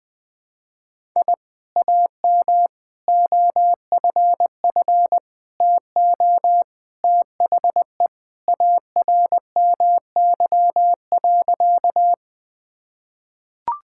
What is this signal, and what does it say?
Morse code, heard as a keyed tone, about 19 wpm, 700 Hz. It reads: I AM OFF TO THE ARMY.
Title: Sentences from Top 1000 Words - Review Part 5 - 20wpm